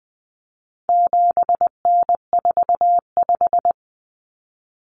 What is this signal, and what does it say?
Morse code, heard as a keyed tone, about 20 wpm, 700 Hz. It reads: 7N45
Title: Common Cut Numbers - 4-Digit 20wpm